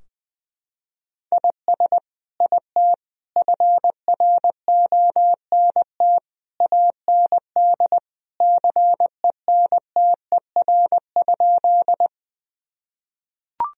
Morse code, heard as a keyed tone, about 20 wpm, 700 Hz. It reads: IS IT FRONT AND CENTER?
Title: Sentences from Top 1000 Words - Review Part 2 - 20wpm